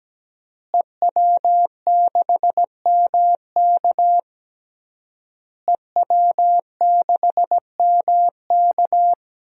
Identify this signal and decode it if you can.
Morse code, heard as a keyed tone, about 17 wpm, 700 Hz. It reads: EW6MK EW6MK
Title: DXCC-100 Call Signs - 5 Characters - Repeated 3x - 17wpm